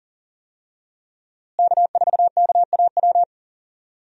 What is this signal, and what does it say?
Morse code, heard as a keyed tone, about 39 wpm, 700 Hz. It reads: K4KAW
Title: USA Call Signs - 5 Characters - Repeated 2x - 40wpm